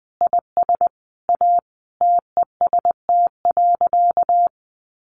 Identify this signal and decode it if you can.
Morse code, heard as a keyed tone, about 20 wpm, 700 Hz. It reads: IS A TEST.
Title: Additional Easy Phrases 20wpm